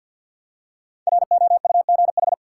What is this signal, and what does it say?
Morse code, heard as a keyed tone, about 51 wpm, 700 Hz. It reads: ROUGH
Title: Five Letter Words - 50wpm